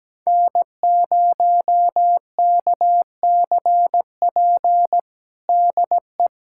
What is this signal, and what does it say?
Morse code, heard as a keyed tone, about 17 wpm, 700 Hz. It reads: N0KCP DE
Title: USA Callsign DE Callsign 17wpm